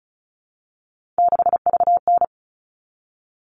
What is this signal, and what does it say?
Morse code, heard as a keyed tone, about 34 wpm, 700 Hz. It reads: 6VN
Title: All Cut Numbers - 3-Digit 35wpm